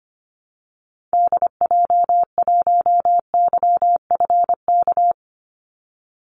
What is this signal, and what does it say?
Morse code, heard as a keyed tone, about 25 wpm, 700 Hz. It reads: DJ1YFK
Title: Call Signs - 25wpm